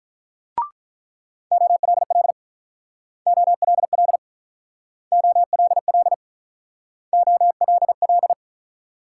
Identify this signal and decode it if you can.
Morse code, heard as a keyed tone, about 47 wpm, 700 Hz. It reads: OLL OLL OLL OLL